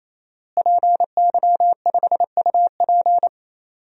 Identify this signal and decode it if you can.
Morse code, heard as a keyed tone, about 27 wpm, 700 Hz. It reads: PY5UP
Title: DXCC-100 Call Signs - 5 Characters - Repeated 3x - 28wpm